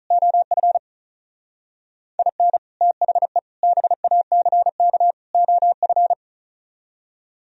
Morse code, heard as a keyed tone, about 44 wpm, 700 Hz. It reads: OF IN THE BACK OF
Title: Speed-Racing - Sets of 4 Words 35wpm